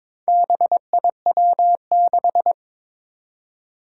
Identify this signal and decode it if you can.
Morse code, heard as a keyed tone, about 22 wpm, 700 Hz. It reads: BIW6
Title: Four Challenging Letter-Number Combinations 22wpm